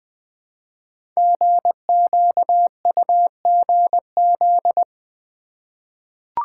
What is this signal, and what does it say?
Morse code, heard as a keyed tone, about 20 wpm, 700 Hz. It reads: GQUGZ
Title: Code Groups - 5 Letters - 20wpm